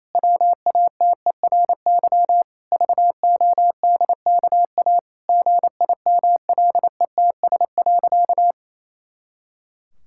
Morse code, heard as a keyed tone, about 28 wpm, 700 Hz. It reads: WATERY VODKA GIMLETS.
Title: Pangram Challenge v2 - 28wpm